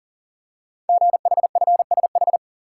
Morse code, heard as a keyed tone, about 42 wpm, 690 Hz. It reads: GHFSH
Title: Code Groups - 5 Letters - 40wpm